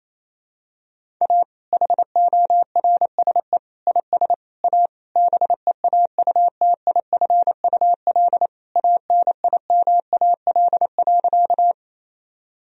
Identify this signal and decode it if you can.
Morse code, heard as a keyed tone, about 28 wpm, 700 Hz. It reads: A HORSE IS A BEAUTIFUL ANIMAL.
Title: Sentences from Top 1000 Words - Review Part 1 - 28wpm